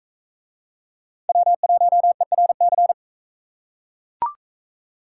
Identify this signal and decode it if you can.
Morse code, heard as a keyed tone, about 42 wpm, 700 Hz. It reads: W1ERC